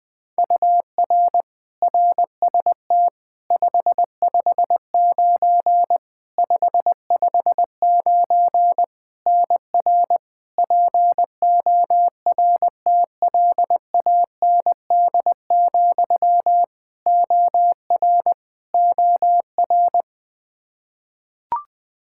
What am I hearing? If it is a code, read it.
Morse code, heard as a keyed tone, about 21 wpm, 700 Hz. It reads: UR RST 559 559 NR PORTLAND, OR OR
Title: Practice QSO - Exchange 1 - Beginning - 20wpm